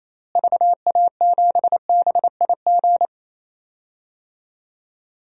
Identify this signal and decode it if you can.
Morse code, heard as a keyed tone, about 28 wpm, 700 Hz. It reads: VA7BIG
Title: DXCC-100 Call Signs - 6 Characters - Repeated 3x - 28wpm